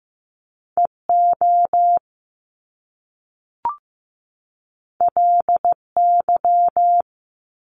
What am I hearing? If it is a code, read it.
Morse code, heard as a keyed tone, about 15 wpm, 700 Hz. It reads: EO LY
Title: Top Two Letter Combinations in Words 15wpm